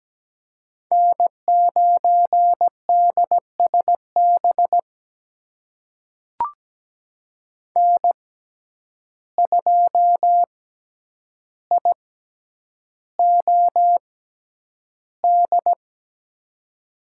Morse code, heard as a keyed tone, about 17 wpm, spaced as for 9 wpm, 700 Hz. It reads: N9DSB N2IOD
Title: USA Call Signs - 5 Characters - 6x Character Spacing - 17wpm